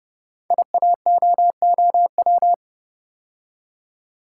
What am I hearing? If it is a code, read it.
Morse code, heard as a keyed tone, about 30 wpm, 700 Hz. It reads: IAOOW
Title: Code Groups - 5 Letters - 30wpm